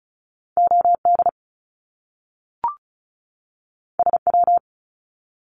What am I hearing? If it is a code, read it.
Morse code, heard as a keyed tone, about 36 wpm, 700 Hz. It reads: OD SW
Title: Top Two Letter Combinations in Words 35wpm